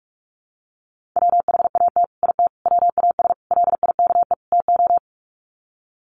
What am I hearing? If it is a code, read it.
Morse code, heard as a keyed tone, about 44 wpm, 700 Hz. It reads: WHAT IT WAS LIKE TO
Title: Sets of 5 Words Encore 45wpm